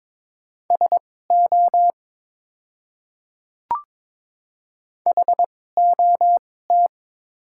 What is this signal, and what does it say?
Morse code, heard as a keyed tone, about 22 wpm, spaced as for 16 wpm, 700 Hz. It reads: SO HOT
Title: 2x Character Spacing - Top 100 Words - 22wpm